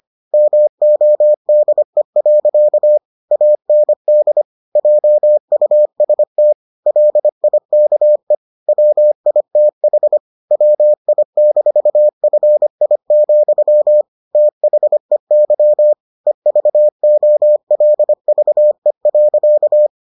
Morse code, heard as a keyed tone, about 25 wpm, 600 Hz. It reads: MODE. AND JUST LIKE WITH WI-FI, THEY EVOLVE.